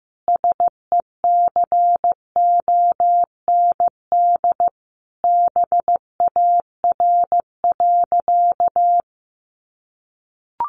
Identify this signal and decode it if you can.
Morse code, heard as a keyed tone, about 15 wpm, 700 Hz. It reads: SECOND BAR.
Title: Easy Phrases 15wpm